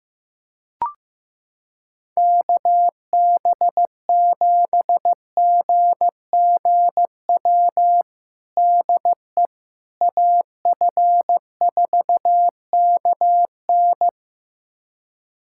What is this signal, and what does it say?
Morse code, heard as a keyed tone, about 15 wpm, 700 Hz. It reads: KB7GGW DE AF4KN
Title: USA Callsign DE Callsign 15wpm